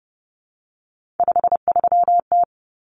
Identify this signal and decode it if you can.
Morse code, heard as a keyed tone, about 30 wpm, 700 Hz. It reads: 53T
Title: Common Cut Numbers - 3-Digit 30wpm